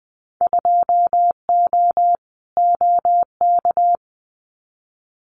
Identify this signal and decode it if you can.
Morse code, heard as a keyed tone, about 20 wpm, 700 Hz. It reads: 2O OK
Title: Winter Field Day Exchanges Encore 20wpm